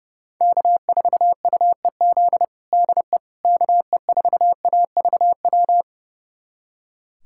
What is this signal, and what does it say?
Morse code, heard as a keyed tone, about 30 wpm, 700 Hz. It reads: K4UEZ DE KE4AVW
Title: USA Callsign DE Callsign 30wpm